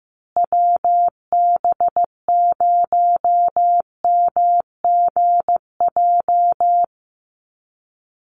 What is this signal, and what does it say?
Morse code, heard as a keyed tone, about 15 wpm, 700 Hz. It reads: WB0MGJ
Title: USA Call Signs Repeated 2x 15wpm